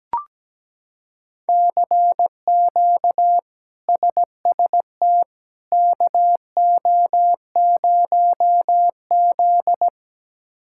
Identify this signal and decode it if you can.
Morse code, heard as a keyed tone, about 17 wpm, 700 Hz. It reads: CQ SST KO0Z
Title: K1USN SST QSO Practice - 17wpm